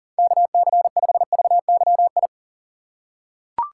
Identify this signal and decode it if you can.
Morse code, heard as a keyed tone, about 40 wpm, 700 Hz. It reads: KC5VYI
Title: USA Call Signs - 6 Characters - 2x Character Spacing - 40wpm